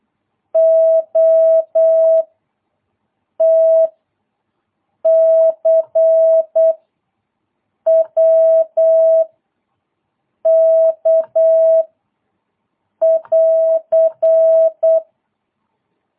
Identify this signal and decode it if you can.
Morse code, heard as a keyed tone, about 8 wpm, 650 Hz. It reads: O T C W K <AR>
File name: jh8cbh / 2022cw